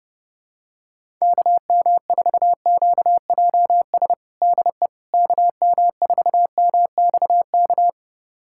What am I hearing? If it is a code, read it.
Morse code, heard as a keyed tone, about 32 wpm, 700 Hz. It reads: KM4QJS DE KM4MXK
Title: USA Callsign DE Callsign - 6 Characters - 30wpm